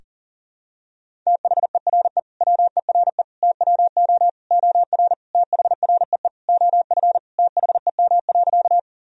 Morse code, heard as a keyed tone, about 40 wpm, 700 Hz. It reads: THERE WERE TWO OR THREE OF THEM.
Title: Sentences from Top 1000 Words - Review Part 1 - 40wpm